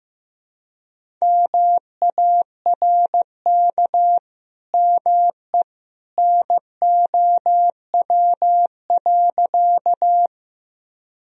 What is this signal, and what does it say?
Morse code, heard as a keyed tone, about 15 wpm, 700 Hz. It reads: MARK ME NOW.
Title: Sentences from Top 1000 Words - Review Part 1 - 15wpm